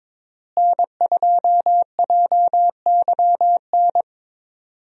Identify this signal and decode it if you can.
Morse code, heard as a keyed tone, about 22 wpm, 700 Hz. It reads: N2JYN
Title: USA Call Signs - 5 Characters - Repeated 3x - 22wpm